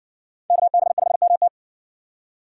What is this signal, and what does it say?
Morse code, heard as a keyed tone, about 60 wpm, 700 Hz. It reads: KB5MT